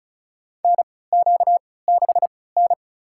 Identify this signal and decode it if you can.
Morse code, heard as a keyed tone, about 34 wpm, spaced as for 20 wpm, 700 Hz. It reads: NQ6N
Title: USA Call Signs - 3x Character Spacing - 35wpm